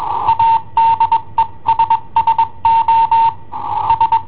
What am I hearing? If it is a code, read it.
Morse code, heard as a keyed tone, about 21 wpm, 910 Hz. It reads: ADESSO S
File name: i7ohp / cassetta i7ohp B